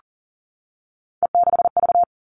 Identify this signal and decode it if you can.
Morse code, heard as a keyed tone, about 40 wpm, 700 Hz. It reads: E6V
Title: All Cut Numbers - 3-Digit 40wpm